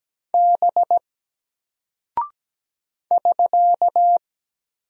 Morse code, heard as a keyed tone, about 17 wpm, 700 Hz. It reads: B <SK>
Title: Single Character (Letter:Number:Punctuation:Prosign) 17wpm